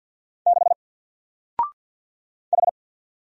Morse code, heard as a keyed tone, about 48 wpm, 700 Hz. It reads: 6 H
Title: Single Letter-Number 50wpm